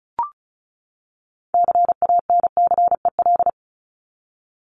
Morse code, heard as a keyed tone, about 34 wpm, 700 Hz. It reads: CANCEL
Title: Top 1001-1250 Words 35wpm